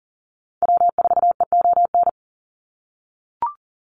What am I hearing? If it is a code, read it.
Morse code, heard as a keyed tone, about 40 wpm, 700 Hz. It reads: W4EON
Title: NA SOTA Activator Callsigns 40wpm